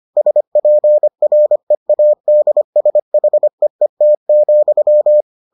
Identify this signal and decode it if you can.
Morse code, heard as a keyed tone, about 25 wpm, 600 Hz. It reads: SPREADSHEET,